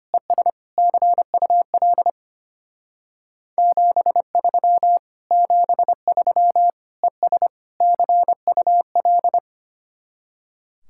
Morse code, heard as a keyed tone, about 31 wpm, 700 Hz. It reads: ES CUL 73 73 ES CUL